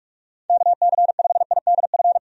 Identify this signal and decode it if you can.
Morse code, heard as a keyed tone, about 46 wpm, 700 Hz. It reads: KC5IDF
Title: USA Call Signs - 6 Characters - 45wpm